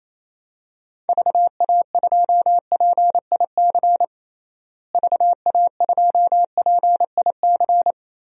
Morse code, heard as a keyed tone, about 28 wpm, 700 Hz. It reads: VA2PIC VA2PIC
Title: DXCC-100 Call Signs - 6 Characters - Repeated 3x - 28wpm